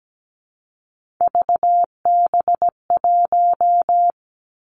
Morse code, heard as a keyed tone, about 17 wpm, 700 Hz. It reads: VB1
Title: Three Challenging Letter-Number Combinations 17wpm